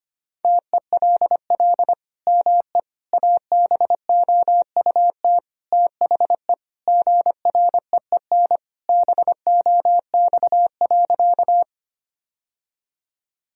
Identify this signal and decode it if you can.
Morse code, heard as a keyed tone, about 25 wpm, 700 Hz. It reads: TELL ME ABOUT THE GREEN BOX.
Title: Sentences from Top 1000 Words - Review Part 2 - 25wpm